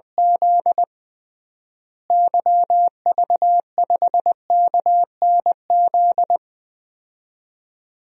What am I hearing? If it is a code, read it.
Morse code, heard as a keyed tone, about 21 wpm, 700 Hz. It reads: Z YV5KNZ
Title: DXCC-100 Call Signs - 6 Characters - Repeated 3x - 20wpm